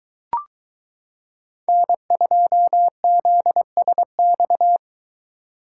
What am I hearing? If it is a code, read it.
Morse code, heard as a keyed tone, about 23 wpm, 700 Hz. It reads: N2ZSX